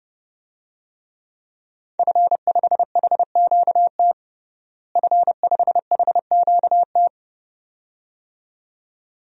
Morse code, heard as a keyed tone, about 30 wpm, 700 Hz. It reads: F5HQT F5HQT
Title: DXCC-100 Call Signs - 5 Characters - Repeated 2x - 30wpm